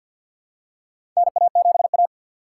Morse code, heard as a keyed tone, about 51 wpm, 700 Hz. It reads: NA7A
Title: USA Call Signs - 4 Characters - 3x Character Spacing - 50wpm